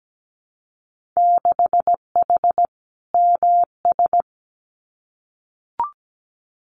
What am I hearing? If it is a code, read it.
Morse code, heard as a keyed tone, about 17 wpm, 700 Hz. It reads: 6H MS
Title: Winter Field Day Exchanges Encore 17wpm